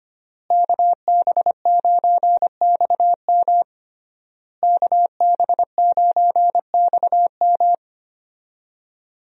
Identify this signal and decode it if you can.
Morse code, heard as a keyed tone, about 25 wpm, 700 Hz. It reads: KB9XM KB9XM
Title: USA Call Signs - 5 Characters - Repeated 3x - 25wpm